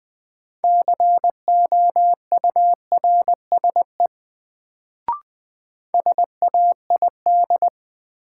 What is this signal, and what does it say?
Morse code, heard as a keyed tone, about 20 wpm, 700 Hz. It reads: COURSE SAID
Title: Sets of 2 Words - New Word Drill - 20wpm